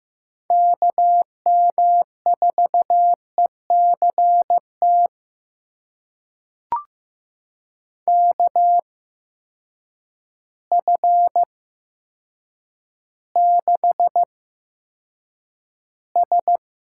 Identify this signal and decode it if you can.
Morse code, heard as a keyed tone, about 15 wpm, spaced as for 8 wpm, 700 Hz. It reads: KM4ECT KF6S
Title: USA Call Signs - 6 Characters - 8x Character Spacing - 15wpm